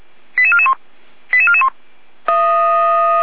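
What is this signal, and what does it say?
Five-tone selective call (ZVEI), heard as quick runs of five tones, tones of about 70 ms. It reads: 88441 88441
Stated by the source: ZVEI five-tone selective call received on an SDR, recorded by fire brigade Feuerwehr Orlamünde.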